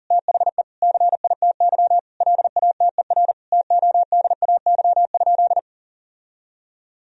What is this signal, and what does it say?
Morse code, heard as a keyed tone, about 40 wpm, 700 Hz. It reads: THE CITY LATER TODAY?
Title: Sentences from Top 1000 Words - Review Part 2 - 40wpm